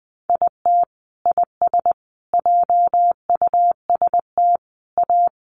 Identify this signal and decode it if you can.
Morse code, heard as a keyed tone, about 20 wpm, 700 Hz. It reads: IT IS JUST A
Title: Sentences from Top 1200 Words 20wpm